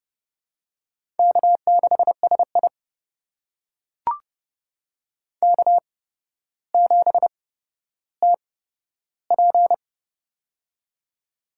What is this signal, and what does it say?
Morse code, heard as a keyed tone, about 30 wpm, spaced as for 13 wpm, 700 Hz. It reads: K6SI K7TP
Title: USA Call Signs - 4 Characters - 8x Character Spacing - 30wpm